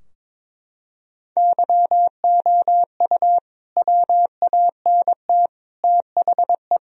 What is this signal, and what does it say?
Morse code, heard as a keyed tone, about 22 wpm, 700 Hz. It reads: YOU WANT THE
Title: Sentences from Top 1000 Words - Review Part 4 - 22wpm